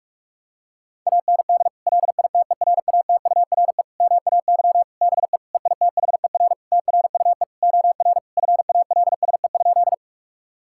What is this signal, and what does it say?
Morse code, heard as a keyed tone, about 46 wpm, 700 Hz. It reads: AND LITERATURE MAY BE EITHER TRUE OR FALSE?
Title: Sentences from Top 1600 Words 45wpm